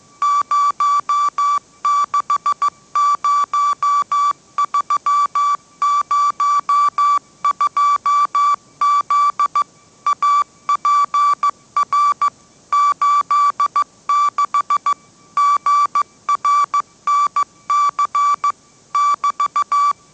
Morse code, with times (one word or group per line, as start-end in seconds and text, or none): 0.2-9.6 060302Z
10.1-12.3 APR
12.7-14.9 86
15.4-18.5 GRNC
18.9-19.9 <BT>